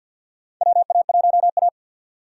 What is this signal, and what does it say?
Morse code, heard as a keyed tone, about 52 wpm, 700 Hz. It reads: WA1A
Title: USA Call Signs - 4 Characters - 3x Character Spacing - 50wpm